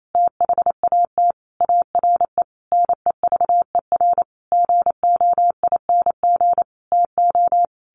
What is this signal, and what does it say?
Morse code, heard as a keyed tone, about 28 wpm, 700 Hz. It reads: THAT ARE NEVER GOING TO
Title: A Sound Of Thunder 28wpm